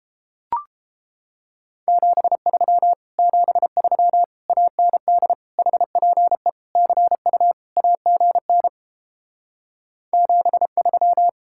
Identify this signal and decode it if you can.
Morse code, heard as a keyed tone, about 29 wpm, 700 Hz. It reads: 73 73 AND HPE CU AGN 73